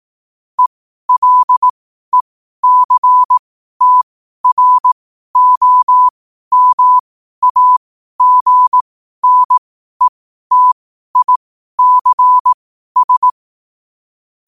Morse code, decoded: ELECTROMAGNETICS

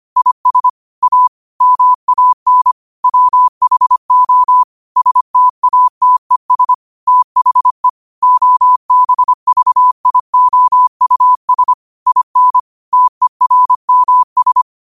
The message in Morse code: IS A MAN WHO STATES THE OBVIOUS IN TERMS

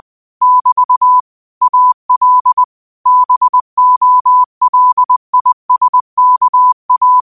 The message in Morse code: <BT> AL BOLISKA

